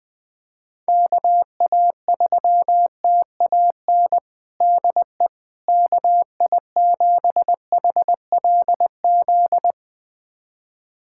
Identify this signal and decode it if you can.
Morse code, heard as a keyed tone, about 21 wpm, 700 Hz. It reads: KA3TAN DE KI7HLZ
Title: USA Callsign DE Callsign - 6 Characters - 20wpm